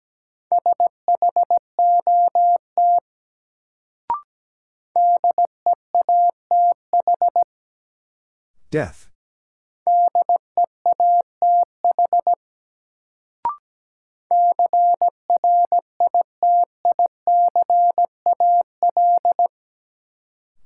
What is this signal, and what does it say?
Morse code, heard as a keyed tone, about 17 wpm, 700 Hz. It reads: SHOT DEATH DEATH CRITICAL
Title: Sentences from Top 1100 Words - New Word Drill Alt - 17wpm